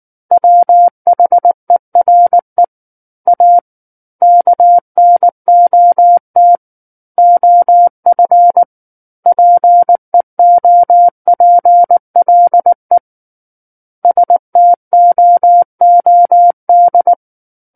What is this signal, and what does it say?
Morse code, heard as a keyed tone, about 19 wpm, 700 Hz. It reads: WHERE A KNOT OF PEOPLE STOOD
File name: Book One / 10-Chapter 10 - 19 WPM